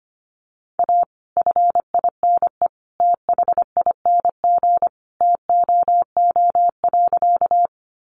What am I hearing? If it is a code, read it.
Morse code, heard as a keyed tone, about 25 wpm, 700 Hz. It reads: A FINE THING TOO.
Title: Sentences from Top 500 Words 25wpm